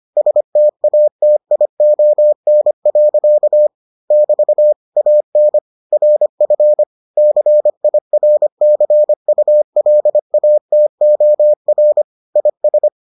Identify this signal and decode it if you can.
Morse code, heard as a keyed tone, about 25 wpm, 600 Hz. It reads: STATION. <BT> AN RF CIRCULATOR IS